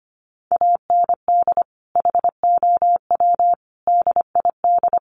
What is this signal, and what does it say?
Morse code, heard as a keyed tone, about 25 wpm, 700 Hz. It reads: AND HOW DID
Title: Sentences from Top 1200 Words 25wpm